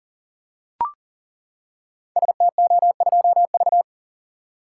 Morse code, heard as a keyed tone, about 40 wpm, 700 Hz. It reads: STO2V
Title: Code Groups - 5 Letter:Numbers - 40wpm